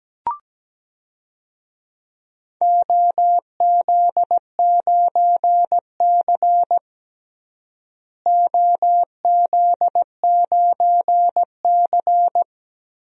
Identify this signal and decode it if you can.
Morse code, heard as a keyed tone, about 17 wpm, 700 Hz. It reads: OZ9C OZ9C
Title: DXCC-100 Call Signs - 4 Characters - Repeated 2x - 17wpm